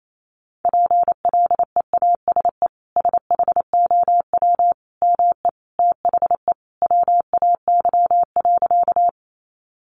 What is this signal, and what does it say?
Morse code, heard as a keyed tone, about 28 wpm, 700 Hz. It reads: PLEASE SHOW ME THE WAY.
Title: Sentences from Top 1500 Words 28wpm